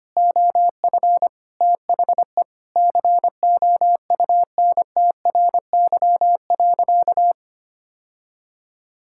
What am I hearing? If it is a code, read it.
Morse code, heard as a keyed tone, about 25 wpm, 700 Hz. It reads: OF THE COUNTRY.
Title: Sentences from Top 1000 Words - Review Part 6 - 25wpm